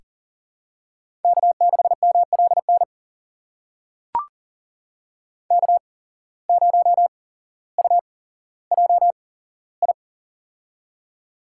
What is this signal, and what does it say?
Morse code, heard as a keyed tone, about 40 wpm, spaced as for 17 wpm, 700 Hz. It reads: K6MLN K0UJI
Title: USA Call Signs - 8x Character Spacing - 40wpm